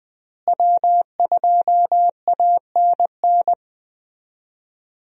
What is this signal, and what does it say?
Morse code, heard as a keyed tone, about 20 wpm, 700 Hz. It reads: W2ANN
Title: USA Call Signs - 5 Characters - Repeated 3x - 20wpm